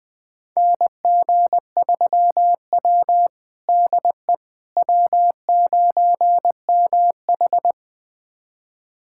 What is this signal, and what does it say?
Morse code, heard as a keyed tone, about 20 wpm, 700 Hz. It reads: NG3W DE W9MH
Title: USA Callsign DE Callsign - 4 Characters - 20wpm